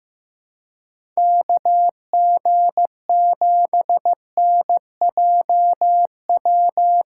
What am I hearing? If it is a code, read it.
Morse code, heard as a keyed tone, about 15 wpm, 700 Hz. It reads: KG7NJW
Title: USA Call Signs - 6 Characters - 3x Character Spacing - 15wpm